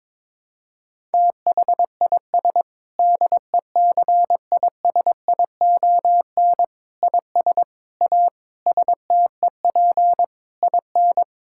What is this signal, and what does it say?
Morse code, heard as a keyed tone, about 22 wpm, 700 Hz. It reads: THIS DECISION IS A STEP IN
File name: Sentences with 7-Word Phrases - 22wpm